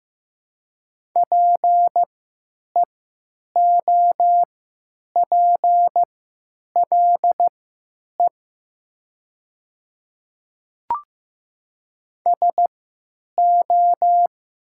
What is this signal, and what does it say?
Morse code, heard as a keyed tone, about 15 wpm, spaced as for 9 wpm, 700 Hz. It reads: PEOPLE SO
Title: 3x Character Spacing - Top 200 Words - 15wpm